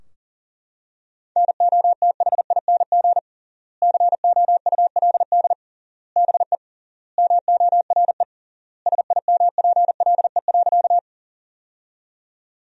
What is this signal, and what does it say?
Morse code, heard as a keyed tone, about 40 wpm, 700 Hz. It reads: NOTHING COULD BE MORE SIMPLE.
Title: 3x Word-Spacing - Sentences from Top 500 Words - 40wpm